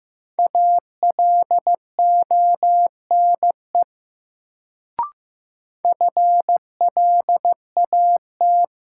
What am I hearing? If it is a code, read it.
Morse code, heard as a keyed tone, about 15 wpm, 700 Hz. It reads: ALONE FLAT
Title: Sentences from Top 1000 Words - New Word Drill - 15wpm